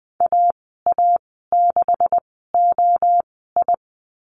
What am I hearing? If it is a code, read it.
Morse code, heard as a keyed tone, about 20 wpm, spaced as for 15 wpm, 700 Hz. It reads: AA6OI
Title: 231 Lesson 37g - Letter x - Call Signs 20wpm